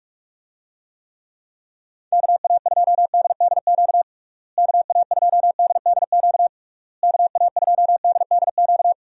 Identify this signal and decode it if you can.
Morse code, heard as a keyed tone, about 46 wpm, 700 Hz. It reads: KA2DDQ KA2DDQ KA2DDQ
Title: USA Call Signs - 6 Characters - Repeated 3x - 45wpm